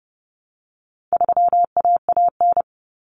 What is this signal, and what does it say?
Morse code, heard as a keyed tone, about 30 wpm, 700 Hz. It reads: 3AAN
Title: Common Cut Numbers - 4-Digit 30wpm